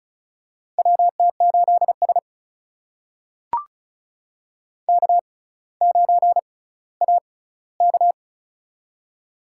Morse code, heard as a keyed tone, about 36 wpm, spaced as for 18 wpm, 700 Hz. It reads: WT8S K9AK
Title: USA Call Signs - 4 Characters - 6x Character Spacing - 35wpm